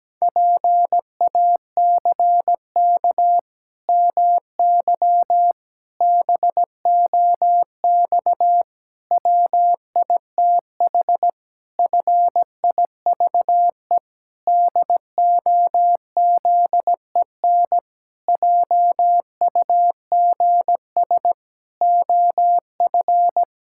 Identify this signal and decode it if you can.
Morse code, heard as a keyed tone, about 17 wpm, 700 Hz. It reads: PACK MY BOX WITH FIVE DOZEN JUGS OF
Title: Pangram Challenge v2 - 17wpm